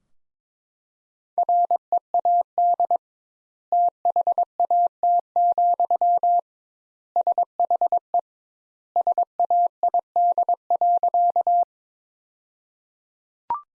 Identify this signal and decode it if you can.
Morse code, heard as a keyed tone, about 22 wpm, 700 Hz. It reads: READ THAT, SHE SAID.
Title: Sentences Top 200 Words - 22wpm